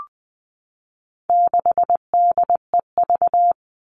Morse code, heard as a keyed tone, about 20 wpm, 700 Hz. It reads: 6DEV
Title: All Cut Numbers - 4-Digit 20wpm